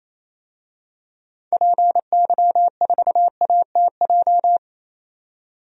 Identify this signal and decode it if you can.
Morse code, heard as a keyed tone, about 27 wpm, 700 Hz. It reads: PY4ATJ
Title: DXCC-100 Call Signs - 6 Characters - 28wpm